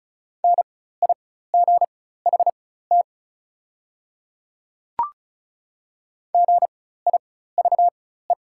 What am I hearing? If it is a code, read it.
Morse code, heard as a keyed tone, about 34 wpm, spaced as for 16 wpm, 690 Hz. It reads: NIGHT GIVE
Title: 4x Character Spacing - Top 300 Words - 35wpm